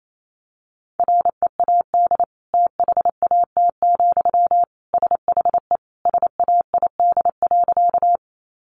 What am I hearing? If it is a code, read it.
Morse code, heard as a keyed tone, about 28 wpm, 700 Hz. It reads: READ THAT, SHE SAID.
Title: Sentences from Top 200 Words 28wpm